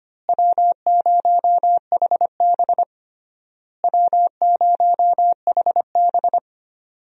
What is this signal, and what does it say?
Morse code, heard as a keyed tone, about 25 wpm, 700 Hz. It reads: W0HB W0HB
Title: USA Call Signs - 4 Characters - Repeated 3x - 25wpm